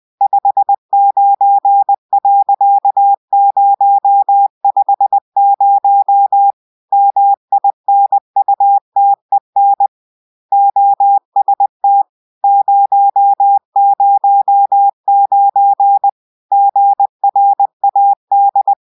Morse code, heard as a keyed tone, about 20 wpm, 800 Hz. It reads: 59.050 MINUTEN OST 009 GRAD